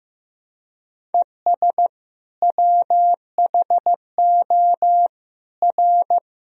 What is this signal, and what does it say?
Morse code, heard as a keyed tone, about 15 wpm, 700 Hz. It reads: ES WHO R
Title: Sentences from Top 100 Words and Abbreviations 15wpm